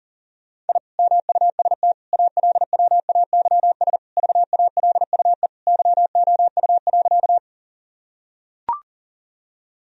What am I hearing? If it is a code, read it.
Morse code, heard as a keyed tone, about 40 wpm, 700 Hz. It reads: I MUST ALWAYS VALUE YOU.
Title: Sentences from Top 1000 Words - Review Part 3 - 40wpm